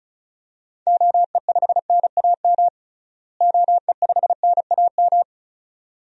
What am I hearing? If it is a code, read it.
Morse code, heard as a keyed tone, about 34 wpm, 700 Hz. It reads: OE5NAM OE5NAM
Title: DXCC-100 Call Signs - 6 Characters - Repeated 2x - 35wpm